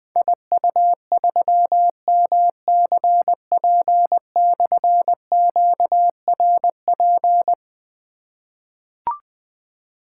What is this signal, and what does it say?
Morse code, heard as a keyed tone, about 20 wpm, 700 Hz. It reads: IU3MCP/QRP
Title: Difficult Call Signs 20wpm